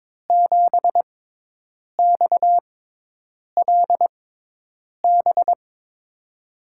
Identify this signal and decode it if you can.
Morse code, heard as a keyed tone, about 22 wpm, spaced as for 8 wpm, 700 Hz. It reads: 7XLB
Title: USA Call Signs - 5 Characters - 6x Character Spacing - 22wpm